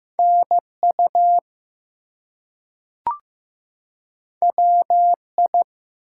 Morse code, heard as a keyed tone, about 15 wpm, 700 Hz. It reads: NU WI
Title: US States and Canadian Provinces 15wpm